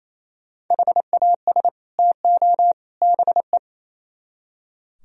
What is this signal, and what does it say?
Morse code, heard as a keyed tone, about 36 wpm, 700 Hz. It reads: HAS TO BE